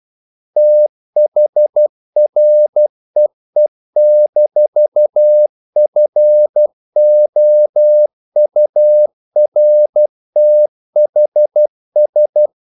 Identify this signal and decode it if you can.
Morse code, heard as a keyed tone, about 12 wpm, 600 Hz. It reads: THREE-FOURTHS
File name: CW / Practice / EnglishQuotations0312